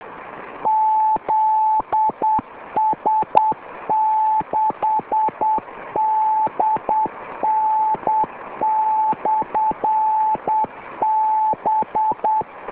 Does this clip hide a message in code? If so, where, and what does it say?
Yes: Morse code from 0.7 to 12.4 s, heard as a keyed tone, about 8 wpm, 870 Hz, with radio noise. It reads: ZS6DN/B